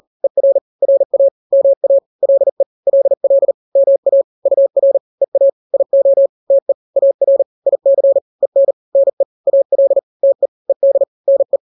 Morse code, heard as a keyed tone, about 40 wpm, 550 Hz. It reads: ER RA MA LE LL MA UR EA IO TE AR IC EN NE AL TE ED NE
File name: n-grams / 2-grams 40 35 WPM